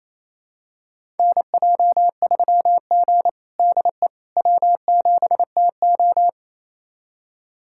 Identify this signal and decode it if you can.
Morse code, heard as a keyed tone, about 28 wpm, 700 Hz. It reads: NJ3G DE W7TO
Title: USA Callsign DE Callsign - 4 Characters - 28wpm